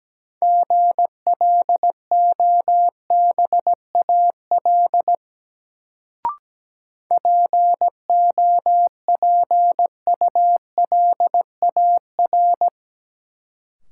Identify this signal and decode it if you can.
Morse code, heard as a keyed tone, about 17 wpm, 700 Hz. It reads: GLOBAL POPULAR
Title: Sentences from Top 1200 Words - New Word Drill Alt - 17wpm